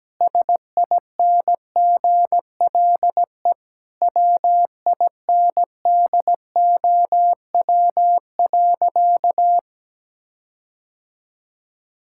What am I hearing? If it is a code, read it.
Morse code, heard as a keyed tone, about 17 wpm, 700 Hz. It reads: SINGLE WINDOW.
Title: Sentences from Top 1600 Words 17wpm